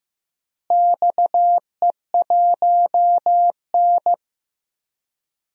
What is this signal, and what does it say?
Morse code, heard as a keyed tone, about 15 wpm, 700 Hz. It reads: XE1N
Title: DXCC-100 Call Signs - 4 Characters - Repeated 2x - 15wpm